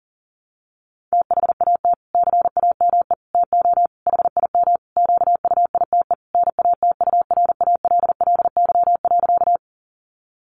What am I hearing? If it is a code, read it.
Morse code, heard as a keyed tone, about 40 wpm, 700 Hz. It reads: THAT CAME TO HIM QUITE NATURALLY.
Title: Sentences from Top 1300 Words 40wpm